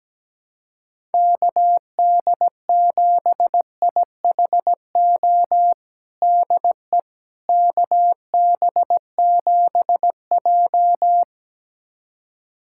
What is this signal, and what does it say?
Morse code, heard as a keyed tone, about 17 wpm, 700 Hz. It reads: KD7IHO DE KB7J
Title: USA Callsign DE Callsign 17wpm